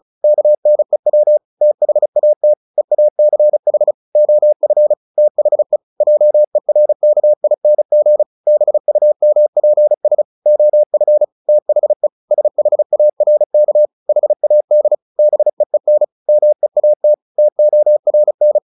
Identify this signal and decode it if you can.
Morse code, heard as a keyed tone, about 36 wpm, 600 Hz. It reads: KNEW THAT EACH OF THE JERKING BUMPS OF THE SHARK HAD BEEN MEAT TORN
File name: the old man and the sea 2 35wpm